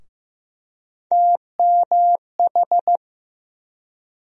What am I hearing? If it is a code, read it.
Morse code, heard as a keyed tone, about 15 wpm, 700 Hz. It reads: TMH
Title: USA Call Sign Postfixes - 3 Letters 15wpm